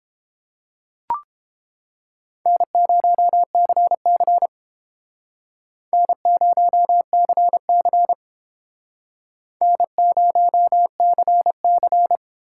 Morse code, heard as a keyed tone, about 31 wpm, 700 Hz. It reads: N0CC N0CC N0CC